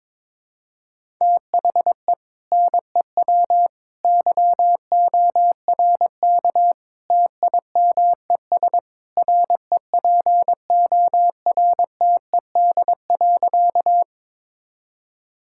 Morse code, heard as a keyed tone, about 22 wpm, 700 Hz. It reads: THE NEW YORK TIMES REPORTED.
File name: practice / 22wpm / Sentences with 3-Word Phrases - 22wpm